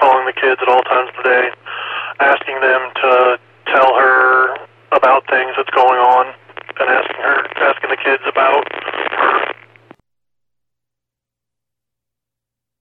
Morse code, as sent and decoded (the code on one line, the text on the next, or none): none